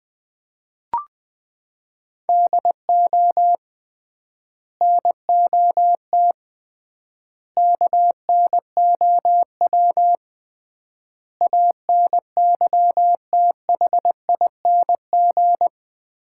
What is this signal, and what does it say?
Morse code, heard as a keyed tone, about 20 wpm, 700 Hz. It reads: DO NOT KNOW ANYTHING
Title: 3x Word-Spacing - Sets of 5 Words - 20wpm